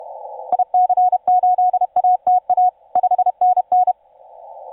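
Morse code, heard as a keyed tone, about 31 wpm, 730 Hz. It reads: IC8ATA 5NN
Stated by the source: TU7C on 20 metre CW, DX station recording by IZ8GNR.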